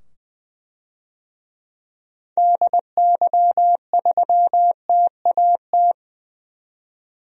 Morse code, heard as a keyed tone, about 20 wpm, 700 Hz. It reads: DY3TAT
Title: DXCC-100 Call Signs - 6 Characters - 20wpm